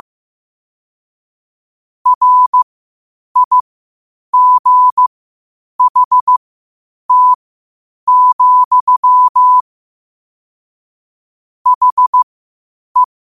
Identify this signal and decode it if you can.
Morse code, heard as a keyed tone, about 15 wpm, spaced as for 8 wpm, 1000 Hz. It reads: RIGHT, HE
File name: qotdfiles / 8wpm